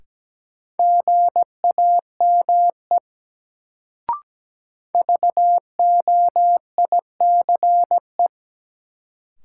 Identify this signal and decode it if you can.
Morse code, heard as a keyed tone, about 17 wpm, 700 Hz. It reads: GAME VOICE
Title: Sentences from Top 500 Words - New Word Drill - 17wpm